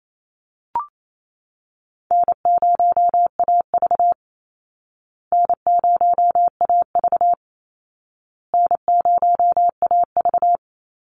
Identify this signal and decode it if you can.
Morse code, heard as a keyed tone, about 28 wpm, 700 Hz. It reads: N0AV N0AV N0AV
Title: Easy Call Signs Repeated 3x - 28wpm